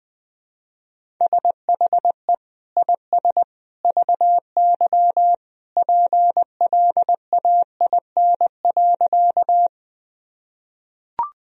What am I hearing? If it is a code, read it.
Morse code, heard as a keyed tone, about 20 wpm, 700 Hz. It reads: SHE IS VY PLAIN.
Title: Sentences from Top 400 Words and Abbreviations 20wpm